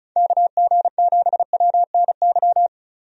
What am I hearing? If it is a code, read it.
Morse code, heard as a keyed tone, about 36 wpm, 700 Hz. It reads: KG7WNY
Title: USA Call Signs - 6 Characters - 35wpm